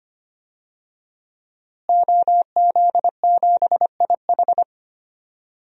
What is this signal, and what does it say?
Morse code, heard as a keyed tone, about 25 wpm, 700 Hz. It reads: OZ7IH
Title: DXCC-100 Call Signs - 5 Characters - Repeated 2x - 25wpm